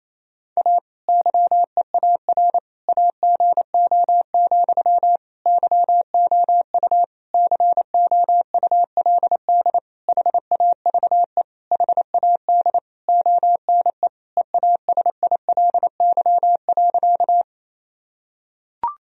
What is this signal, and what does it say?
Morse code, heard as a keyed tone, about 28 wpm, 700 Hz. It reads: A YEAR AGO, YOU COULD HAVE HAD ONE EASILY.
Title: Sentences from Top 1000 Words - Review Part 3 - 28wpm